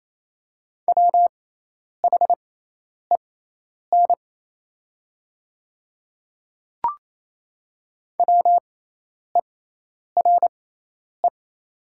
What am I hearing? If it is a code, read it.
Morse code, heard as a keyed tone, about 28 wpm, spaced as for 10 wpm, 700 Hz. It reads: WHEN WERE
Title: 6x Character Spacing - Top 100 Words - 28wpm